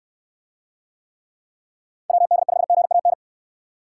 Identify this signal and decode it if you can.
Morse code, heard as a keyed tone, about 68 wpm, 700 Hz. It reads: WD5GAN